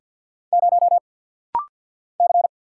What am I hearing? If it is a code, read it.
Morse code, heard as a keyed tone, about 50 wpm, 700 Hz. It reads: 0 C
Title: Single Letter-Number Alt - Rapid-Fire 50wpm